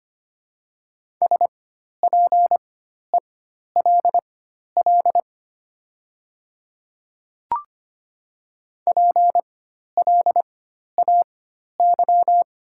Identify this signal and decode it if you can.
Morse code, heard as a keyed tone, about 25 wpm, spaced as for 12 wpm, 700 Hz. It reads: SPELL PLAY
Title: 4x Character Spacing - Top 300 Words - 25wpm